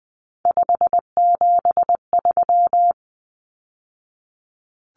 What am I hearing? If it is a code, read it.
Morse code, heard as a keyed tone, about 20 wpm, 700 Hz. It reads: 573
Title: Numbers - 3-Digit 20wpm